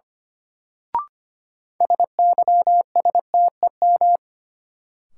Morse code, heard as a keyed tone, about 25 wpm, 700 Hz. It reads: SYSTEM
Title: Sentences from Top 1000 Words - Review Part 3 - New Word Drill - 25wpm